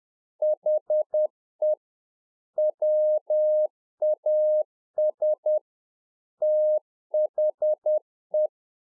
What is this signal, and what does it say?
Morse code, heard as a keyed tone, about 10 wpm, 610 Hz. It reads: HE WAS THE
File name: UNFOGETTABLE QSO 10WPM